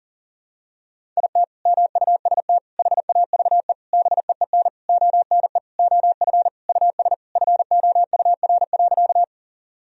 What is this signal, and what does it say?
Morse code, heard as a keyed tone, about 40 wpm, 700 Hz. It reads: IT MUST HAVE BEEN ONE OF US FOUR.
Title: Sentences from Top 1000 Words - Review Part 1 - 40wpm